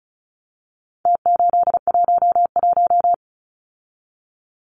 Morse code, heard as a keyed tone, about 34 wpm, 700 Hz. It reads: T811
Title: Numbers - Contest Style - 4-Digit 35wpm